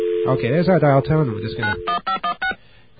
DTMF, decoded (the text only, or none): 91516